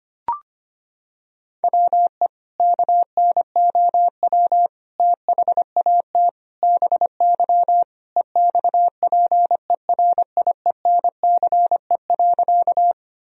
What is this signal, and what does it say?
Morse code, heard as a keyed tone, about 25 wpm, 700 Hz. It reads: WE KNOW THAT BY EXPERIENCE.
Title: Sentences from Top 1000 Words - Review Part 3 - 25wpm